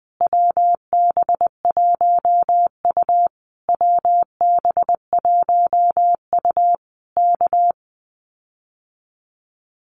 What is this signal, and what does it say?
Morse code, heard as a keyed tone, about 20 wpm, 700 Hz. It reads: WB1U WB1U K
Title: Common QSO Phrases Encore 20wpm